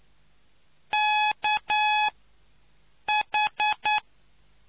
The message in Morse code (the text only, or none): KH